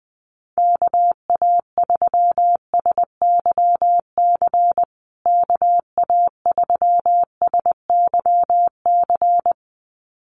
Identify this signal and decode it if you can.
Morse code, heard as a keyed tone, about 20 wpm, 700 Hz. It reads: KA3SYC KA3SYC
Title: USA Call Signs Repeated 2x 20wpm